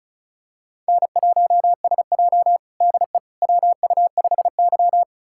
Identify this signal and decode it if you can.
Morse code, heard as a keyed tone, about 36 wpm, 690 Hz. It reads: N1SJ DE WU5Y
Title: USA Callsign DE Callsign - 4 Characters - 35wpm